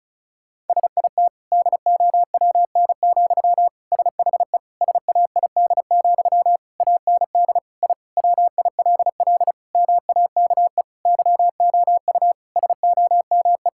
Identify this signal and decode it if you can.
Morse code, heard as a keyed tone, about 36 wpm, 700 Hz. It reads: SIT DOWN, SHE SAID, AND I WILL MAKE YOU SOME
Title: Sentences from Top 1000 Words - Review Part 6 - 35wpm